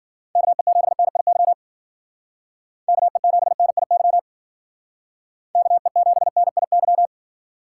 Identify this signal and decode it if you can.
Morse code, heard as a keyed tone, about 49 wpm, 700 Hz. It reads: KE7NIY KE7NIY KE7NIY
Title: Speed Racing - USA 2x3 Call Signs 40wpm